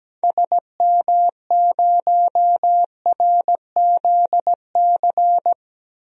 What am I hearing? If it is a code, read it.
Morse code, heard as a keyed tone, about 17 wpm, 700 Hz. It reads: SM0RZC
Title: DXCC-100 Call Signs - 6 Characters - Repeated 2x - 17wpm